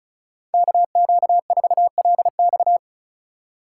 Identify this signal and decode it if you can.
Morse code, heard as a keyed tone, about 34 wpm, 700 Hz. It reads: KQ4LX
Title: USA Call Signs - 5 Characters - 4x Character Spacing - 35wpm